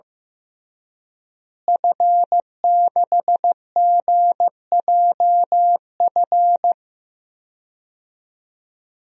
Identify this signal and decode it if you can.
Morse code, heard as a keyed tone, about 15 wpm, 700 Hz. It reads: F6GJF
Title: DXCC-100 Call Signs - 5 Characters - Repeated 3x - 15wpm